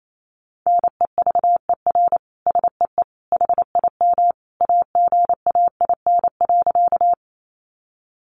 Morse code, heard as a keyed tone, about 28 wpm, 700 Hz. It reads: NEVER SEE HIM AGAIN.
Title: Sentences from Top 300 Words 28wpm